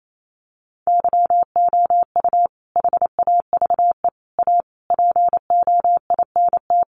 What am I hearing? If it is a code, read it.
Morse code, heard as a keyed tone, about 28 wpm, 700 Hz. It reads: YOU HAVE A POINT
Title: Sentences from Top 1400 Words 28wpm